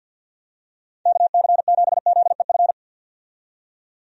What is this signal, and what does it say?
Morse code, heard as a keyed tone, about 52 wpm, 700 Hz. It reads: KC7ZEF
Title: USA Call Signs - 6 Characters - 50wpm